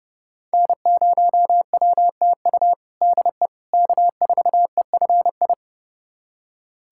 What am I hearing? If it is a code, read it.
Morse code, heard as a keyed tone, about 30 wpm, 700 Hz. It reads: N0WTU DE K4EFI
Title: USA Callsign DE Callsign - 5 Characters - 30wpm